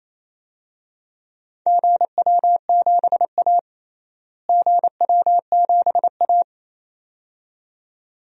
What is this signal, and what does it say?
Morse code, heard as a keyed tone, about 27 wpm, 700 Hz. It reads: GW7A GW7A
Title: DXCC-100 Call Signs - 4 Characters - Repeated 2x - 28wpm